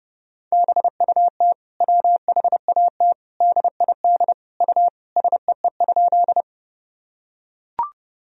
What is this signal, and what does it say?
Morse code, heard as a keyed tone, about 30 wpm, 700 Hz. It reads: BUT WHAT DID U SEE?
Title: Sentences from Top 100 Words and Abbreviations 30wpm